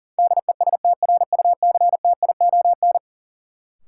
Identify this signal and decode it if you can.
Morse code, heard as a keyed tone, about 40 wpm, 700 Hz. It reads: DESTRUCTION
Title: Sets of 5 Words Encore - New Word Drill - 40wpm